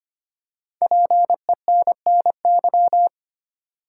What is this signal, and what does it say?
Morse code, heard as a keyed tone, about 25 wpm, 700 Hz. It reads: PENNY
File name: 5 Letter Words - First 3 Letters Given - 25wpm